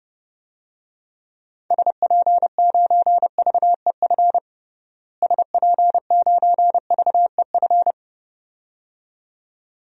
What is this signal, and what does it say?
Morse code, heard as a keyed tone, about 29 wpm, 700 Hz. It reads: SP9VEF SP9VEF
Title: DXCC-100 Call Signs - 6 Characters - Repeated 2x - 30wpm